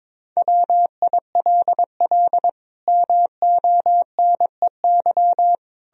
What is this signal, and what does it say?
Morse code, heard as a keyed tone, about 22 wpm, 700 Hz. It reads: WILL MONEY
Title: Sentences from Top 1000 Words - Review Part 4 - 22wpm